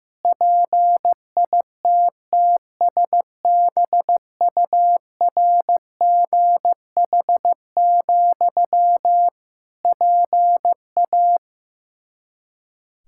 Morse code, heard as a keyed tone, about 15 wpm, 700 Hz. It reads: PITTSBURGH, PA
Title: Practice QSO - Exchange 1 - Beginning - 15wpm